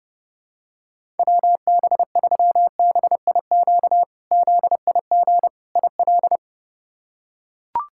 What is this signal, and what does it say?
Morse code, heard as a keyed tone, about 30 wpm, 700 Hz. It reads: WB3BIQ ZIG IL
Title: K1USN SST QSO Practice - 30wpm